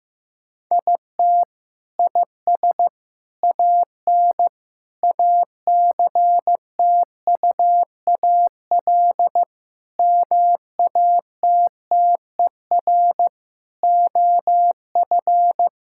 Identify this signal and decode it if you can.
Morse code, heard as a keyed tone, about 16 wpm, 700 Hz. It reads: IT IS AN ACTUAL MATTER OF
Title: Sentences from Top 1000 Words - Review Part 5 - 15wpm